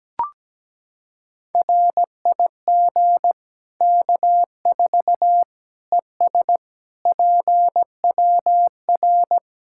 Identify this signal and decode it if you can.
Morse code, heard as a keyed tone, about 17 wpm, 700 Hz. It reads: RIG K4 ES PWR
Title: Practice QSO - Exchange 2 - Info - 17wpm